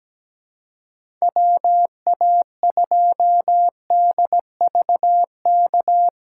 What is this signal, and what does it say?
Morse code, heard as a keyed tone, about 17 wpm, 700 Hz. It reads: WA2DVK
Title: USA Call Signs - 6 Characters - 3x Character Spacing - 17wpm